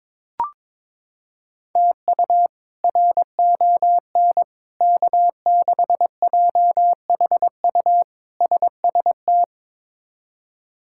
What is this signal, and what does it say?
Morse code, heard as a keyed tone, about 22 wpm, 700 Hz. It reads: TU RON K6JHU SST
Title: K1USN SST QSO Practice - 22wpm